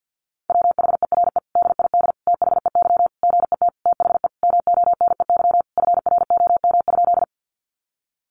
Morse code, heard as a keyed tone, about 50 wpm, 700 Hz. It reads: WHERE DID THEY GET THE MONEY FROM?
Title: Sentences from Top 400 Words 50wpm